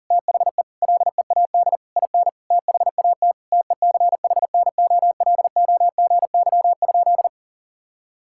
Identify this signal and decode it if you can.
Morse code, heard as a keyed tone, about 40 wpm, 700 Hz. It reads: THE LEAD IN THAT TECHNOLOGY?
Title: Sentences from Top 1000 Words - Review Part 5 - 40wpm